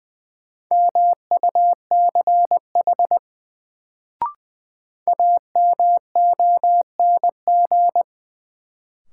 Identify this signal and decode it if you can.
Morse code, heard as a keyed tone, about 21 wpm, 700 Hz. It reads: MUCH AMONG
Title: Sets of 2 Words - New Word Drill - 20wpm